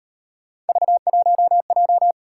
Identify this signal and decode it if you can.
Morse code, heard as a keyed tone, about 38 wpm, 700 Hz. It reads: V1J